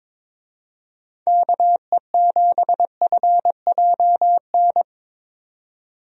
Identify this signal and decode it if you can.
Morse code, heard as a keyed tone, about 22 wpm, 700 Hz. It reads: KE7FJN
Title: USA Call Signs - 6 Characters - 22wpm